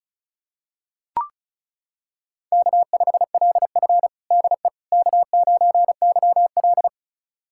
Morse code, heard as a keyed tone, about 36 wpm, 700 Hz. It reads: K5LF DE K9YL
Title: USA Callsign DE Callsign - 4 Characters - 35wpm